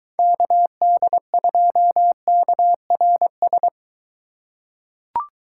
Morse code, heard as a keyed tone, about 23 wpm, 700 Hz. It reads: KD2KRS